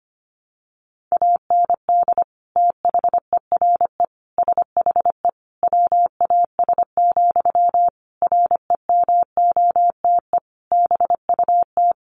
Morse code, heard as a keyed tone, about 25 wpm, 700 Hz. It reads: AND THERE SHE WAS, REMOTE BUT
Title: Sentences from Top 1300 Words 25wpm